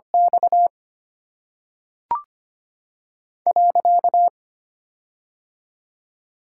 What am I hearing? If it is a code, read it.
Morse code, heard as a keyed tone, about 25 wpm, 700 Hz. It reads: X .
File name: Top 90 QSO Elements - 8x Character Spacing - 25wpm